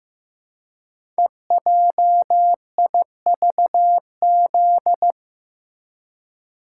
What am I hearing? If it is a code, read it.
Morse code, heard as a keyed tone, about 15 wpm, 700 Hz. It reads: EJIVZ
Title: Code Groups - 5 Letters - 15wpm